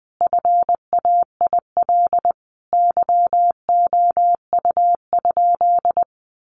Morse code, heard as a keyed tone, about 20 wpm, 700 Hz. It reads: FAIL YOU?
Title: Sentences from Top 1300 Words 20wpm